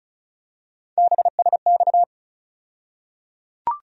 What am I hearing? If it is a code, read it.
Morse code, heard as a keyed tone, about 34 wpm, 700 Hz. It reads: BSX
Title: USA Call Sign Postfixes - 3 Letters 35wpm